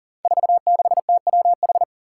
Morse code, heard as a keyed tone, about 39 wpm, 700 Hz. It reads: 46TWH
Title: Five Challenging Letter-Number Combinations 40wpm